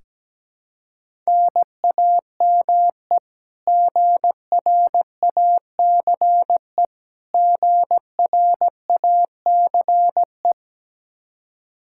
Morse code, heard as a keyed tone, about 17 wpm, 700 Hz. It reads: NAME GRACE GRACE
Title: Practice QSO - Exchange 1 - Beginning - 17wpm